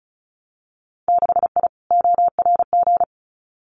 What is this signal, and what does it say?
Morse code, heard as a keyed tone, about 36 wpm, 700 Hz. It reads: 6I ORG
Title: Winter Field Day Exchanges Encore 35wpm